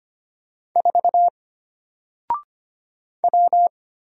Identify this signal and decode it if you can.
Morse code, heard as a keyed tone, about 25 wpm, 700 Hz. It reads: 4 W
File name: Single-Letter-Number-25wpm